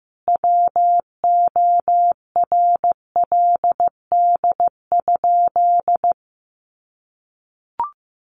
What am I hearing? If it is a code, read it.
Morse code, heard as a keyed tone, about 14 wpm, 700 Hz. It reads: WORLD?
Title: Sentences from Top 900 Words 15wpm